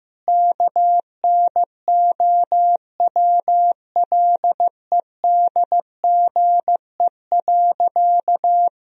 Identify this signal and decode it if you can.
Morse code, heard as a keyed tone, about 15 wpm, 700 Hz. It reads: KNOWLEDGE.
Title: Sentences from Top 1000 Words - Review Part 3 - 15wpm